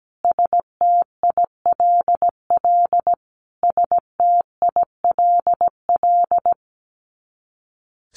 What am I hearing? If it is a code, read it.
Morse code, heard as a keyed tone, about 17 wpm, 700 Hz. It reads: STILL STILL
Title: Top 300 Words Repeated 17wpm